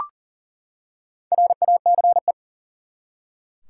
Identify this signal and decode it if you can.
Morse code, heard as a keyed tone, about 40 wpm, 700 Hz. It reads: RACE
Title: Four Letter Words - 40wpm